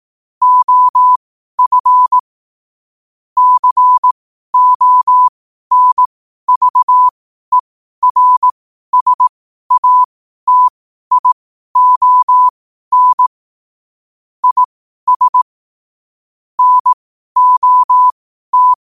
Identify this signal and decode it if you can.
Morse code, heard as a keyed tone, about 18 wpm, spaced as for 12 wpm, 1000 Hz. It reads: OF CONVERSATION IS NOT
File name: qotdfiles / 12wpm